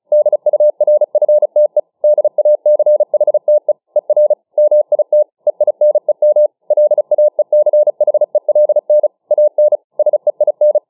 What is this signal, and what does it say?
Morse code, heard as a keyed tone, about 36 wpm, 600 Hz. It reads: DURFTE DACHTE ER MIT EINEM LAECHELN AN SEIN